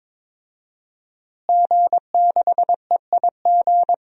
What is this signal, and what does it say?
Morse code, heard as a keyed tone, about 22 wpm, 700 Hz. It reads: G6EIG
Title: DXCC-100 Call Signs - 5 Characters - Repeated 3x - 22wpm